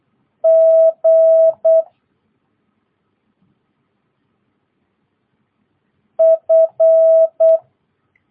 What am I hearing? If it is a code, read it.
Morse code, heard as a keyed tone, about 8 wpm, 650 Hz. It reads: G F
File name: jh8cbh / 2022cw